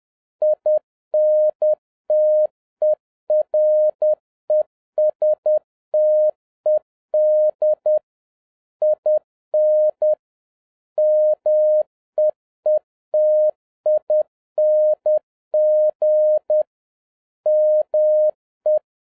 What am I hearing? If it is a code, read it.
Morse code, heard as a keyed tone, about 10 wpm, 610 Hz. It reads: INTERESTED IN MEETING ME
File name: 2011-07-mp3 / n46 JOHN K3TN 10wpm